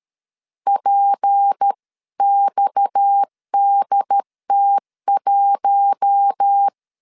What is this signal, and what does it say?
Morse code, heard as a keyed tone, about 13 wpm, 800 Hz. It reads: PXDT1
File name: CW / DET / 1 6 1500C S A 12wpm 3440